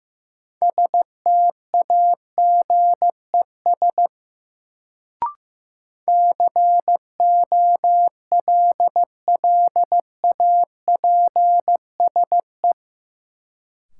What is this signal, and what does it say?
Morse code, heard as a keyed tone, about 15 wpm, 700 Hz. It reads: STAGES COLLAPSE
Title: Sets of 5 Words Encore - New Word Drill - 15wpm